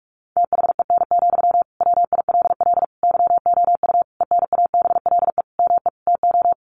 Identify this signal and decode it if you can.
Morse code, heard as a keyed tone, about 46 wpm, 700 Hz. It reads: THEN, WILL YOU ENABLE ME TO
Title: Sentences from Top 1500 Words 45wpm